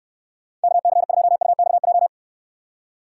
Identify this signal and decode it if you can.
Morse code, heard as a keyed tone, about 69 wpm, 700 Hz. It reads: KC2UXJ